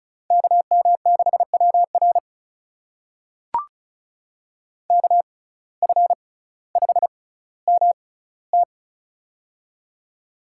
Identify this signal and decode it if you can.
Morse code, heard as a keyed tone, about 36 wpm, spaced as for 18 wpm, 700 Hz. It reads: KM6WR KF5MT
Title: USA Call Signs - 5 Characters - 6x Character Spacing - 35wpm